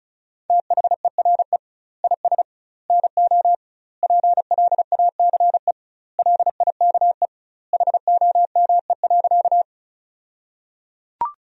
Morse code, heard as a keyed tone, about 34 wpm, 700 Hz. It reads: THERE IS NO PLACE LIKE HOME.
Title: 2x Word-Spacing - Sentences from Top 200 Words - 35wpm